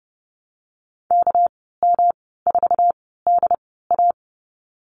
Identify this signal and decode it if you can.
Morse code, heard as a keyed tone, about 30 wpm, spaced as for 17 wpm, 700 Hz. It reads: KM4DA
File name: Lesson 37g - Letter x - Call Signs 30wpm